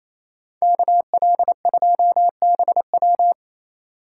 Morse code, heard as a keyed tone, about 34 wpm, 700 Hz. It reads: KL2BW
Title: Speed Racing - USA 2x2 Call Signs 28wpm